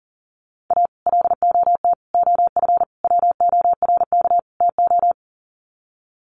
Morse code, heard as a keyed tone, about 40 wpm, 700 Hz. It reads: A LOT OF WORK TO
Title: Sets of 5 Words Encore2 40wpm